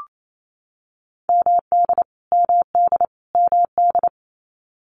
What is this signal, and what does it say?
Morse code, heard as a keyed tone, about 28 wpm, 700 Hz. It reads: MD MD MD
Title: US States Repeated 3x 28wpm